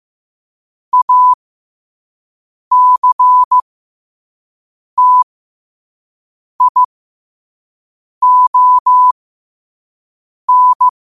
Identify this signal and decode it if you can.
Morse code, heard as a keyed tone, about 15 wpm, spaced as for 5 wpm, 1000 Hz. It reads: ACTION
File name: qotdfiles / 5wpm